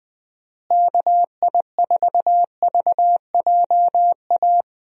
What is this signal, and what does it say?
Morse code, heard as a keyed tone, about 20 wpm, 700 Hz. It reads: KI4VJA
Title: USA Call Signs - 6 Characters - Repeated 2x - 20wpm